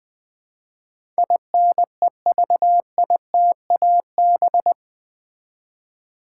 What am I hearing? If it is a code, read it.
Morse code, heard as a keyed tone, about 20 wpm, 700 Hz. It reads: INEVITAB
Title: Build A Word - 4 Letters At A Time - 10-Letter Words - 20wpm